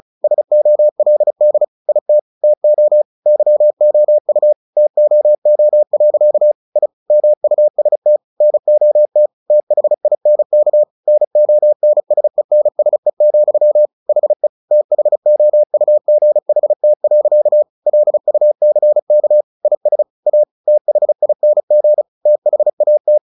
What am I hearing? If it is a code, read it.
Morse code, heard as a keyed tone, about 34 wpm, 600 Hz. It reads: SOLD IT TO YOU TOO. I MUST NOT THINK NONSENSE, HE THOUGHT. LUCK IS A THING THAT
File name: the old man and the sea 2 35wpm